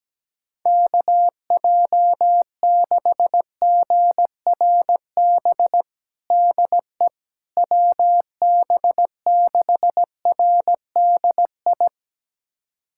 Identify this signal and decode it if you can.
Morse code, heard as a keyed tone, about 17 wpm, 700 Hz. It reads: KJ6GRB DE WB6RDI
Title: USA Callsign DE Callsign - 6 Characters - 17wpm